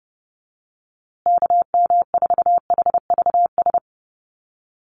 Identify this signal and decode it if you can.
Morse code, heard as a keyed tone, about 30 wpm, 700 Hz. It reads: KM4HVS
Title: USA Call Signs Repeated 2x 30wpm